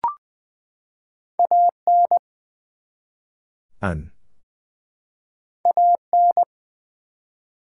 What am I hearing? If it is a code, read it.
Morse code, heard as a keyed tone, about 20 wpm, 700 Hz. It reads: AN AN
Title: Two Letter Words - 20wpm